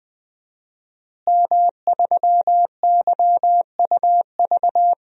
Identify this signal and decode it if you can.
Morse code, heard as a keyed tone, about 20 wpm, 700 Hz. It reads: M3YUV
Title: DXCC-100 Call Signs - 5 Characters - Repeated 2x - 20wpm